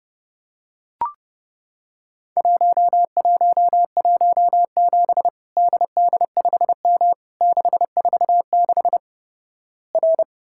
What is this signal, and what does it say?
Morse code, heard as a keyed tone, about 28 wpm, 700 Hz. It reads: 1117 DD5M 646 R
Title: Worked All Europe DX Contest - QTC Exchange - 30wpm